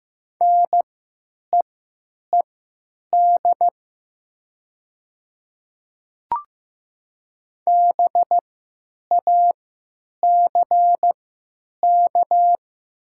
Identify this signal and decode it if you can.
Morse code, heard as a keyed tone, about 15 wpm, spaced as for 9 wpm, 700 Hz. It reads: NEED BACK
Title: 3x Character Spacing - Top 300 Words - 15wpm